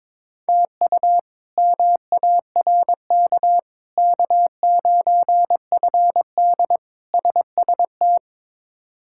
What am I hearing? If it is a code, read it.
Morse code, heard as a keyed tone, about 22 wpm, 700 Hz. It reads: TU MARK K9FD SST
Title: K1USN SST QSO Practice - 22wpm